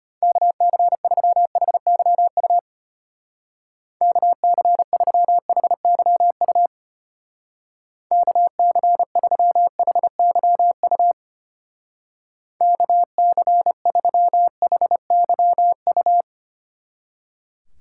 Morse code, heard as a keyed tone, about 33 wpm, 700 Hz. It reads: KC3HYU KC3HYU KC3HYU KC3HYU